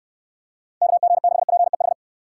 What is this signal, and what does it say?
Morse code, heard as a keyed tone, about 69 wpm, 700 Hz. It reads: KG6PEH